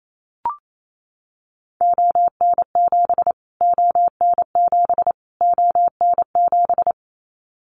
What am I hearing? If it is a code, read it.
Morse code, heard as a keyed tone, about 28 wpm, 700 Hz. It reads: ON7 ON7 ON7
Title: Callsign Prefixes for Top 25 DXCC - Repeated 3x - 28wpm